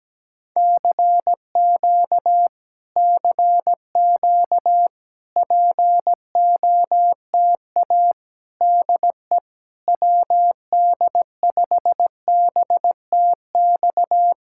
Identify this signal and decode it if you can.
Morse code, heard as a keyed tone, about 17 wpm, 700 Hz. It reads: CQ CQ POTA DE WD5BTX
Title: Speed-Racing Common QSO Phrases Encore - 17wpm